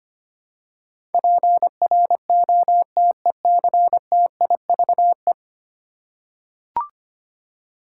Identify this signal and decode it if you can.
Morse code, heard as a keyed tone, about 25 wpm, 700 Hz. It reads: PROTECTIVE
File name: 10 Letter Words - First 3 Letters Given - 25wpm